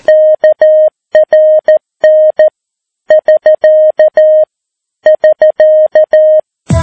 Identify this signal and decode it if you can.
Morse code, heard as a keyed tone, about 14 wpm, 620 Hz. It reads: KRN <SK> <SK>
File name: PI4AA Uitzending